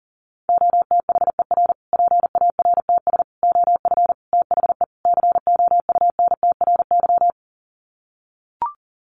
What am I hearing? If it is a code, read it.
Morse code, heard as a keyed tone, about 40 wpm, 700 Hz. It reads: OTHER PARTS OF THE COUNTRY
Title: Sets of 5 Words Encore 40wpm